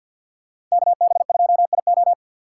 Morse code, heard as a keyed tone, about 51 wpm, 700 Hz. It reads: KB2IO
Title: USA Call Signs - 5 Characters - 8x Character Spacing - 50wpm